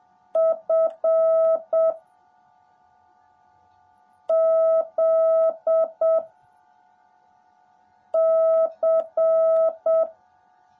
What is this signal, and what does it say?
Morse code, heard as a keyed tone, about 7 wpm, 650 Hz. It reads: F Z C
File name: jh8cbh / 2021cw / cw18a